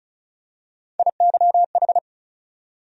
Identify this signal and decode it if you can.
Morse code, heard as a keyed tone, about 36 wpm, 700 Hz. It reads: IYH
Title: USA Call Sign Postfixes - 3 Letters 35wpm